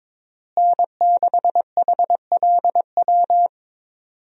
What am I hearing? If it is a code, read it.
Morse code, heard as a keyed tone, about 22 wpm, 700 Hz. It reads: N6HLW
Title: USA Call Signs - 5 Characters - Repeated 3x - 22wpm